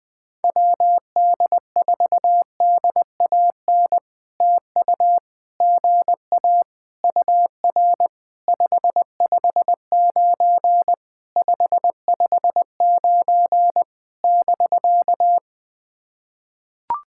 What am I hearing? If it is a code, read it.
Morse code, heard as a keyed tone, about 20 wpm, 700 Hz. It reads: WD4DAN TU GA UR 559 559 <BK>
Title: POTA QSO Practice - 20wpm